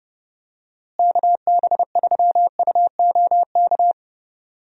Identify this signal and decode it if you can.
Morse code, heard as a keyed tone, about 31 wpm, 700 Hz. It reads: KB3UOK
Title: USA Call Signs - 6 Characters - Repeated 3x - 30wpm